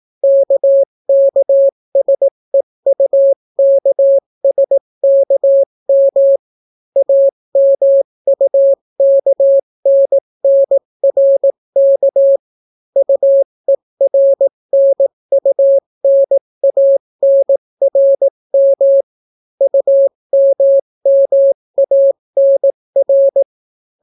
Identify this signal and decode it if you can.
Morse code, heard as a keyed tone, about 18 wpm, 550 Hz. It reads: KKSEUKSKM AMUKNNRK UERNUNANRM UMMANR